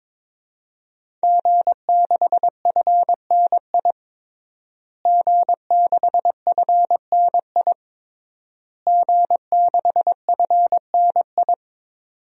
Morse code, heard as a keyed tone, about 22 wpm, 700 Hz. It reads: G6FNI G6FNI G6FNI
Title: DXCC-100 Call Signs - 5 Characters - Repeated 3x - 22wpm